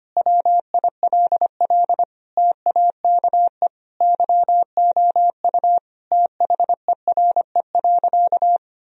Morse code, heard as a keyed tone, about 25 wpm, 700 Hz. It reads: WILL TAKE YOU THERE.
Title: Sentences from Top 1000 Words - Review Part 3 - 25wpm